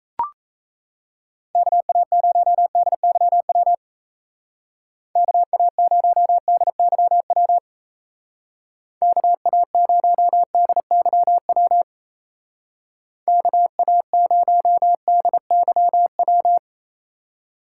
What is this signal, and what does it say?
Morse code, heard as a keyed tone, about 34 wpm, 700 Hz. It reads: KA0DYW KA0DYW KA0DYW KA0DYW
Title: Speed Racing - USA Call Signs 28wpm